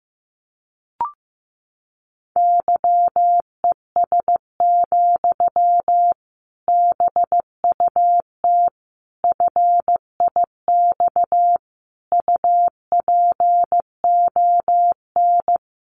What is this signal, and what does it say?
Morse code, heard as a keyed tone, about 15 wpm, 700 Hz. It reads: YES, BUT FIX UPON
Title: Sentences from Top 1400 Words 15wpm